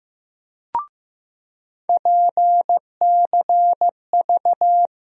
Morse code, heard as a keyed tone, about 15 wpm, 700 Hz. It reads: PCV
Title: USA Call Sign Postfixes - 3 Letters 15wpm